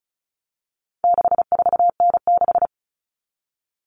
Morse code, heard as a keyed tone, about 34 wpm, 700 Hz. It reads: 64N6
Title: All Cut Numbers - 4-Digit 35wpm